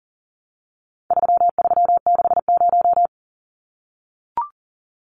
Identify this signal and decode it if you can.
Morse code, heard as a keyed tone, about 40 wpm, 700 Hz. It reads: 3360
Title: Numbers - 4-Digit 40wpm